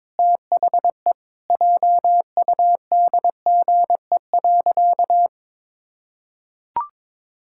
Morse code, heard as a keyed tone, about 21 wpm, 700 Hz. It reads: THE JUDGE.
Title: Pangram Challenge v2 - 22wpm